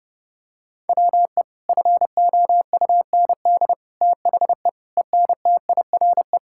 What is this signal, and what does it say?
Morse code, heard as a keyed tone, about 30 wpm, 700 Hz. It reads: WE FOUND THE ENTIRE
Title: Sentences from Top 1000 Words - Review Part 5 - 30wpm